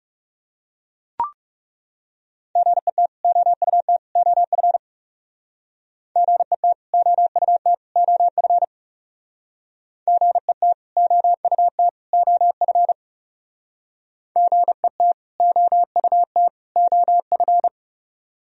Morse code, heard as a keyed tone, about 37 wpm, 690 Hz. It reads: GET OUT OF GET OUT OF GET OUT OF GET OUT OF
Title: Speed-Racing - Sets of 3 Words 30wpm